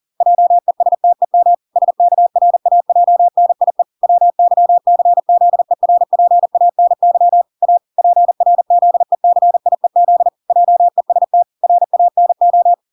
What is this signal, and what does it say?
Morse code, heard as a keyed tone, about 40 wpm, 700 Hz. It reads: JESTEM SKRAJNIE WYCZERPANY A PRZECIEZ JEST RANO